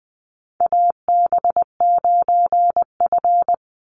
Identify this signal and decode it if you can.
Morse code, heard as a keyed tone, about 20 wpm, 700 Hz. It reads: AB9F
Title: USA Call Signs 20wpm